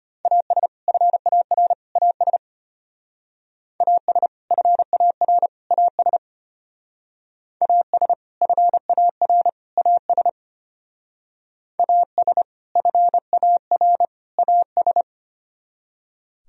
Morse code, heard as a keyed tone, about 33 wpm, 700 Hz. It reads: AS FAR AS AS FAR AS AS FAR AS AS FAR AS